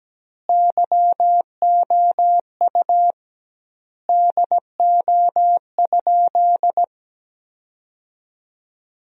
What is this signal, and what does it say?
Morse code, heard as a keyed tone, about 17 wpm, 700 Hz. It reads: YOU DO?
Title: 2x Word-Spacing - Sentences from Top 100 Words - 17wpm